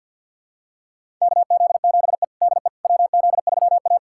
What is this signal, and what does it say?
Morse code, heard as a keyed tone, about 47 wpm, 700 Hz. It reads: KZ7E DE WZ3A
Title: USA Callsign DE Callsign - 4 Characters - 50wpm